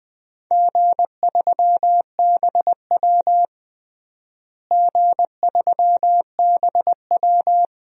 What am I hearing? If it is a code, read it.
Morse code, heard as a keyed tone, about 19 wpm, 700 Hz. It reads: G3BW G3BW
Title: DXCC-100 Call Signs - 4 Characters - Repeated 3x - 20wpm